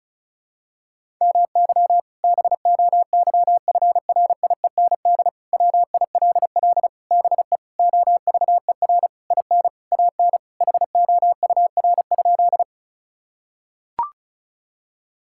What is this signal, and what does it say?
Morse code, heard as a keyed tone, about 34 wpm, 700 Hz. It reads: MY BOYFRIEND WILL BE OVER IN AN HOUR?
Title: Sentences from Top 1000 Words - Review Part 5 - 35wpm